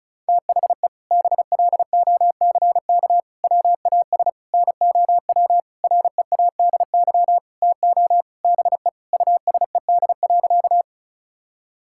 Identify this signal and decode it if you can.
Morse code, heard as a keyed tone, about 36 wpm, 700 Hz. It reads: THE BLOCK WAS NOW READY TO BE USED.
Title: Sentences from Top 1600 Words 35wpm